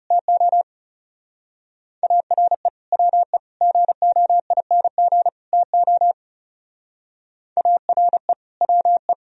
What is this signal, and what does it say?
Morse code, heard as a keyed tone, about 39 wpm, 700 Hz. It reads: TO ARE WE GOING TO ARE WE